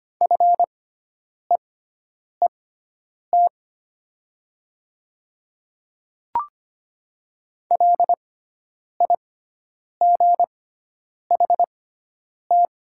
Morse code, heard as a keyed tone, about 25 wpm, spaced as for 9 wpm, 700 Hz. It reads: FEET LIGHT
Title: 6x Character Spacing - Top 300 Words - 25wpm